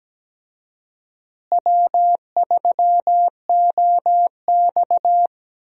Morse code, heard as a keyed tone, about 17 wpm, 700 Hz. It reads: W3OX
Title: USA Call Signs - 4 Characters - Repeated 2x - 17wpm